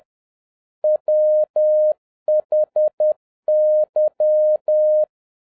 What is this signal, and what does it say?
Morse code, heard as a keyed tone, about 10 wpm, 610 Hz. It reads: WHY
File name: 2011-07-mp3 / n41 EXCITING QRP CW OPERATION 10wpm